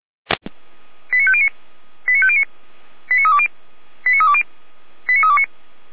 Five-tone selective call (ZVEI), heard as quick runs of five tones, tones of about 70 ms. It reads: 89509 89509 89320 89320 89329